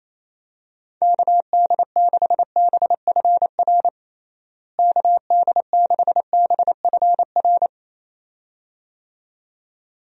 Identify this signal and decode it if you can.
Morse code, heard as a keyed tone, about 27 wpm, 700 Hz. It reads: KD6BFR KD6BFR
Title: USA Call Signs - 6 Characters - Repeated 2x - 28wpm